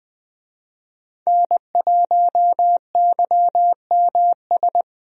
Morse code, heard as a keyed tone, about 20 wpm, 700 Hz. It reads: N1YMS
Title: USA Call Signs - 5 Characters - 20wpm